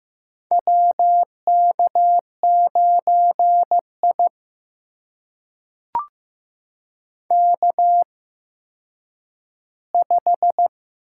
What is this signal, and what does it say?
Morse code, heard as a keyed tone, about 15 wpm, spaced as for 9 wpm, 700 Hz. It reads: WK9I K5
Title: USA Call Signs - 4 Characters - 8x Character Spacing - 15wpm